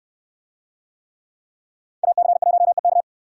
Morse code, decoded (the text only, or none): AC1EK